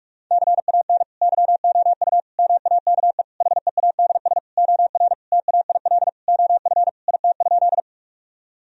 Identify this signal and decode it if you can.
Morse code, heard as a keyed tone, about 41 wpm, 700 Hz. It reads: CAN YOU MAKE HEADS OR TAIL OF IT?
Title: Sentences from Top 1000 Words - Review Part 3 - 45wpm